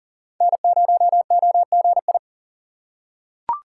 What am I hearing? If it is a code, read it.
Morse code, heard as a keyed tone, about 40 wpm, 700 Hz. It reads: N0OGI
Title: USA Call Signs - 5 Characters - 3x Character Spacing - 40wpm